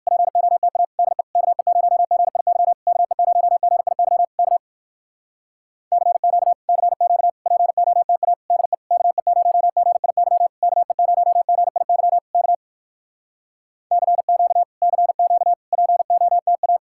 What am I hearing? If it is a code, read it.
Morse code, heard as a keyed tone, about 56 wpm, 700 Hz. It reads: POTA DE KE0GIY KE0GIY K CQ CQ POTA DE KE0GIY KE0GIY K CQ CQ POTA